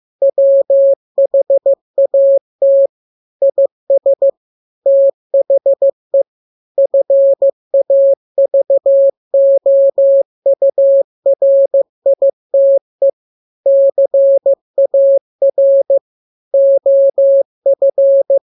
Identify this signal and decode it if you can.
Morse code, heard as a keyed tone, about 15 wpm, 550 Hz. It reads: WHAT IS THE FAVOURITE CAR OF
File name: jokes / joke 13 15 0000